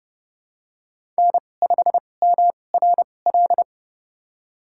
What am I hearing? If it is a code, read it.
Morse code, heard as a keyed tone, about 30 wpm, spaced as for 22 wpm, 700 Hz. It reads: N5MRL
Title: USA Call Signs - 5 Characters - 2x Character Spacing - 30wpm